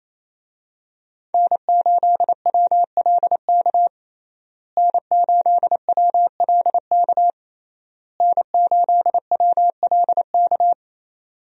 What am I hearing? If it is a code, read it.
Morse code, heard as a keyed tone, about 27 wpm, 700 Hz. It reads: N8WLK N8WLK N8WLK
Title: USA Call Signs - 5 Characters - Repeated 3x - 28wpm